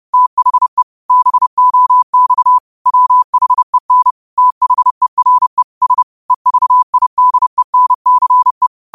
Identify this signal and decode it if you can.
Morse code, heard as a keyed tone, about 30 wpm, 1000 Hz. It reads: THE BOX WHEN THERE S EVIDENCE
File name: qotdfiles / 30wpm